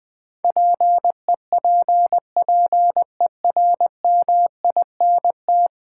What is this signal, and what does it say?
Morse code, heard as a keyed tone, about 20 wpm, 700 Hz. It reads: PEPPERMINT
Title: Difficult Words with Hints - 20wpm